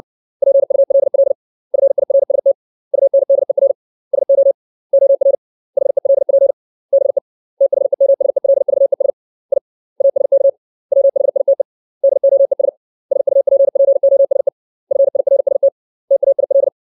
Morse code, heard as a keyed tone, about 60 wpm, 550 Hz. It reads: PULL FIRST UNDER SO OR HELP BE THEMSELVES I ASK WHITE DOES SUPPOSE FIRST TAIL
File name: 632 words / 632 Words 60WPM